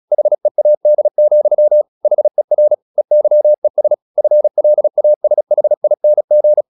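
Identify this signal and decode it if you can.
Morse code, heard as a keyed tone, about 36 wpm, 600 Hz. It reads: HEAD, HER EYES FLASHING
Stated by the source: Morse practice at 36 wpm, text of Ken Follett's novel Never, from CW med Gnister.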